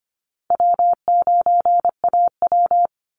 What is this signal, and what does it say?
Morse code, heard as a keyed tone, about 25 wpm, 700 Hz. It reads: W9AW
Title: Easy Call Signs - 25wpm